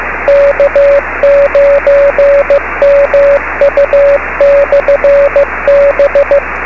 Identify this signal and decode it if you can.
Morse code, heard as a keyed tone, about 15 wpm, 560 Hz, with radio noise. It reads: K9MU/B
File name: beacons / k9mu